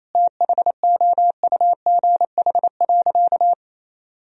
Morse code, heard as a keyed tone, about 27 wpm, 700 Hz. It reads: THOUGH.
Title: Sentences from Top 1000 Words - Review Part 2 - 28wpm